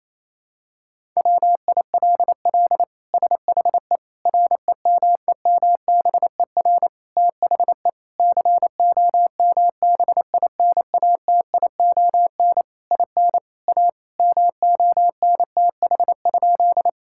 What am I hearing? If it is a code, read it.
Morse code, heard as a keyed tone, about 28 wpm, 700 Hz. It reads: WILL SHE REMEMBER THE COMBINATION IN A MONTH?
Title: Sentences from Top 1000 Words - Review Part 6 - 28wpm